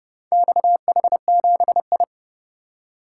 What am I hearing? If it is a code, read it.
Morse code, heard as a keyed tone, about 27 wpm, 700 Hz. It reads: XH7I